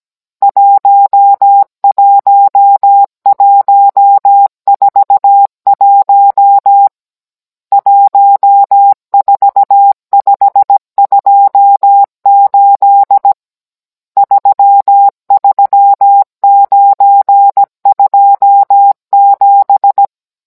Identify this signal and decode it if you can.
Morse code, heard as a keyed tone, about 17 wpm, 800 Hz. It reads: 11141 14528 33927
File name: Cours-NR18-16mm